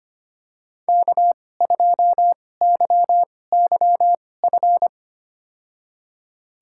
Morse code, heard as a keyed tone, about 25 wpm, spaced as for 18 wpm, 700 Hz. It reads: K2YYF
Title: USA Call Signs - 5 Characters - 2x Character Spacing - 25wpm